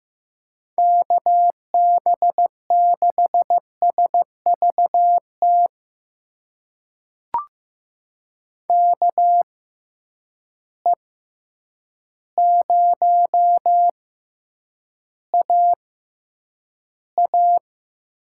Morse code, heard as a keyed tone, about 15 wpm, spaced as for 8 wpm, 700 Hz. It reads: KB6SVT KE0AA
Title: USA Call Signs - 6 Characters - 6x Character Spacing - 15wpm